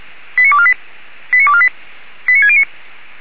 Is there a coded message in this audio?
Yes: five-tone selective call (ZVEI) from 0.4 to 2.6 s, heard as quick runs of five tones, tones of about 70 ms. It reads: 89248 89248 89609